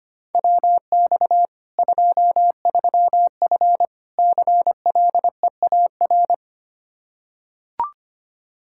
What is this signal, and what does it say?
Morse code, heard as a keyed tone, about 25 wpm, 700 Hz. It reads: WX 23F CLEAR
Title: K1USN SST QSO Practice 25wpm